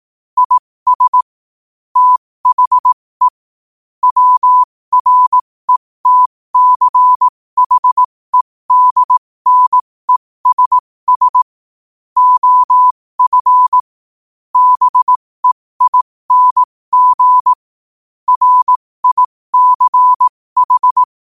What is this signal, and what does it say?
Morse code, heard as a keyed tone, about 18 wpm, spaced as for 15 wpm, 1000 Hz. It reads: IS THE WRETCHEDNESS OF BEING RICH
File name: qotdfiles / 15wpm